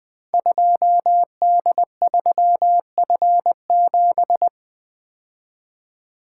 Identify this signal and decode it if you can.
Morse code, heard as a keyed tone, about 20 wpm, 700 Hz. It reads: 2D3F7
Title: Code Groups - 5 Letter:Numbers - 20wpm